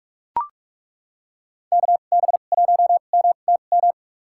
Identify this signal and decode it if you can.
Morse code, heard as a keyed tone, about 43 wpm, spaced as for 32 wpm, 700 Hz. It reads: KB1MTM
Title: USA Call Signs - 6 Characters - 2x Character Spacing - 45wpm